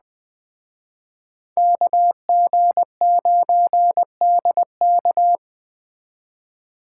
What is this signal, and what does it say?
Morse code, heard as a keyed tone, about 20 wpm, 700 Hz. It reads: KG9DK
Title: Speed Racing - USA 2x2 Call Signs 17wpm